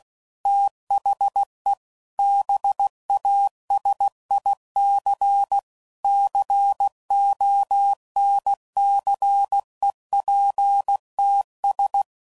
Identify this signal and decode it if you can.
Morse code, heard as a keyed tone, about 16 wpm, 800 Hz. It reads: THE BASIC CONCEPTS
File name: CW Code Groups @15WPM / Technical Subjects / Gates Intro